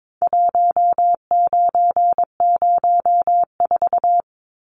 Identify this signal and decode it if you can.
Morse code, heard as a keyed tone, about 22 wpm, 700 Hz. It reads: 1904
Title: Numbers - 4-Digit 22wpm